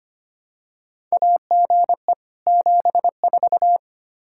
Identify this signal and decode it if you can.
Morse code, heard as a keyed tone, about 25 wpm, 700 Hz. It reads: AGE 74
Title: Practice QSO - Exchange 3 - More Info - 25wpm